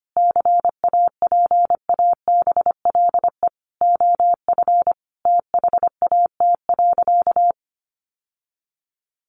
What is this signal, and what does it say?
Morse code, heard as a keyed tone, about 25 wpm, 700 Hz. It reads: CAPABLE OF THAT.
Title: Sentences from Top 1200 Words 25wpm